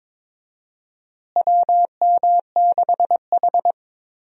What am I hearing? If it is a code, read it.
Morse code, heard as a keyed tone, about 22 wpm, 700 Hz. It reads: WM6H
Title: USA Call Signs - 4 Characters - Repeated 2x - 22wpm